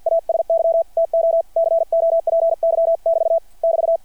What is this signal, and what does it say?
Morse code, heard as a keyed tone, about 53 wpm, 660 Hz. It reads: ASY TO COPY <BT> <BT>
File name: archived / CRSnet